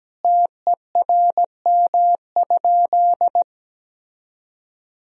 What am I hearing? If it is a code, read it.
Morse code, heard as a keyed tone, about 17 wpm, 700 Hz. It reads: TERM?
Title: Sentences from Top 1000 Words - Review Part 4 - 17wpm